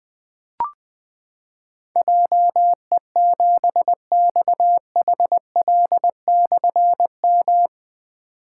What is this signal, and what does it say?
Morse code, heard as a keyed tone, about 20 wpm, 700 Hz. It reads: JE7XHL/M
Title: Difficult Call Signs 20wpm